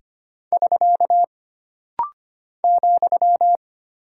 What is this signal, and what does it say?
Morse code, heard as a keyed tone, about 25 wpm, 700 Hz. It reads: <SK> ,
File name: Single Punctuation-Prosign - Rapid-Fire - 25wpm